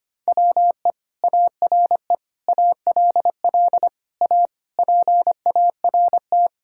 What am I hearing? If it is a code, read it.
Morse code, heard as a keyed tone, about 25 wpm, 700 Hz. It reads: WE ARE ALL A PART
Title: Sentences from Top 1000 Words - Review Part 6 - 25wpm